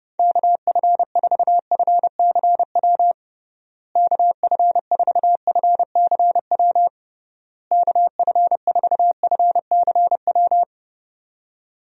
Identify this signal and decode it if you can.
Morse code, heard as a keyed tone, about 30 wpm, 700 Hz. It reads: KF4FCW KF4FCW KF4FCW
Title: USA Call Signs - 6 Characters - Repeated 3x - 30wpm